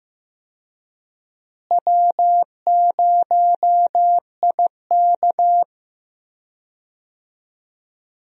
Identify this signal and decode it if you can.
Morse code, heard as a keyed tone, about 15 wpm, 700 Hz. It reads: W0IK
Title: USA Call Signs - 4 Characters - Repeated 3x - 15wpm